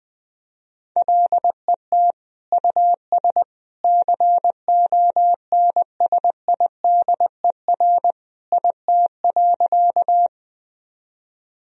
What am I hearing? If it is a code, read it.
Morse code, heard as a keyed tone, about 20 wpm, 700 Hz. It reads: LET US CONSIDER IT.
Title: Sentences from Top 1000 Words - Review Part 4 - 20wpm